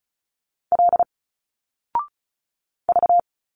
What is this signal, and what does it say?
Morse code, heard as a keyed tone, about 34 wpm, 700 Hz. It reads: L V
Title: Single Letters 35wpm